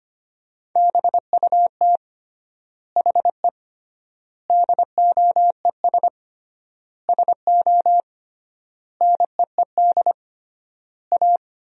Morse code, heard as a keyed tone, about 25 wpm, 700 Hz. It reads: BUT HE DOES SO NEED A
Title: 3x Word-Spacing - Sentences from Top 200 Words - 25wpm